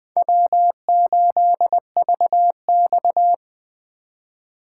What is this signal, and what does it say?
Morse code, heard as a keyed tone, about 21 wpm, 700 Hz. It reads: W8VX